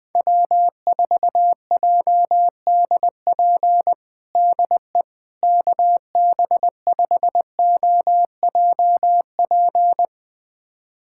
Morse code, heard as a keyed tone, about 20 wpm, 700 Hz. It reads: W4JDP DE KB5OJP
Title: USA Callsign DE Callsign 20wpm